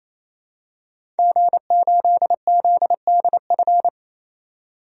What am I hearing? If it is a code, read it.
Morse code, heard as a keyed tone, about 27 wpm, 700 Hz. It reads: G8ZDF
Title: DXCC-100 Call Signs - 5 Characters - 28wpm